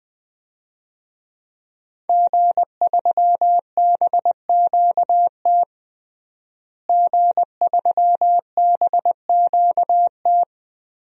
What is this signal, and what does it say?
Morse code, heard as a keyed tone, about 19 wpm, 700 Hz. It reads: G3BQT G3BQT
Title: DXCC-100 Call Signs - 5 Characters - Repeated 3x - 20wpm